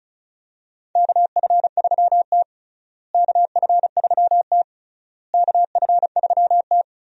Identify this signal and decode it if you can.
Morse code, heard as a keyed tone, about 36 wpm, 700 Hz. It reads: KF3T KF3T KF3T
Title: USA Call Signs - 4 Characters - Repeated 3x - 35wpm